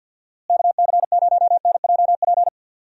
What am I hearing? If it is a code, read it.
Morse code, heard as a keyed tone, about 51 wpm, 700 Hz. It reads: KC0NJP
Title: USA Call Signs - 6 Characters - Repeated 2x - 50wpm